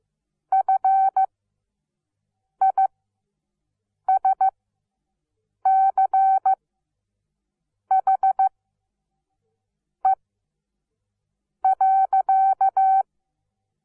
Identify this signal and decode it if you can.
Morse code, heard as a keyed tone, about 15 wpm, spaced as for 5 wpm, 770 Hz. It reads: FISCHE.
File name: Onlinelehrgang / morsen / DJ4UF Morsen17